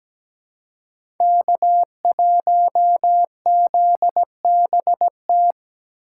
Morse code, heard as a keyed tone, about 17 wpm, 700 Hz. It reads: K1ZBT
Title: USA Call Signs - 5 Characters - 8x Character Spacing - 17wpm